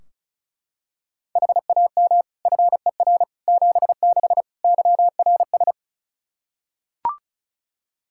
Morse code, heard as a keyed tone, about 35 wpm, 700 Hz. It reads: HAM FER 76 YRS
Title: K1USN SST QSO Practice 35wpm